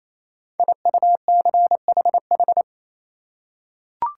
Morse code, heard as a keyed tone, about 27 wpm, 700 Hz. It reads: IUCHH
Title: Code Groups - 5 Letters - 28wpm